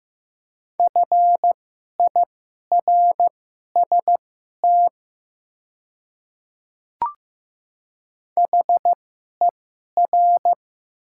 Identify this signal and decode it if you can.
Morse code, heard as a keyed tone, about 15 wpm, spaced as for 11 wpm, 700 Hz. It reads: FIRST HER
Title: 2x Character Spacing - Top 100 Words - 15wpm